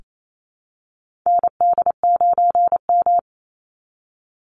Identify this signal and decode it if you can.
Morse code, heard as a keyed tone, about 28 wpm, 700 Hz. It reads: ND9M
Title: Call Signs Repeated 2x - 28wpm